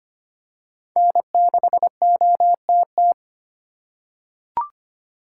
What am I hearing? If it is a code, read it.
Morse code, heard as a keyed tone, about 25 wpm, 700 Hz. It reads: N6OTT
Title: USA Call Signs - 5 Characters - 4x Character Spacing - 25wpm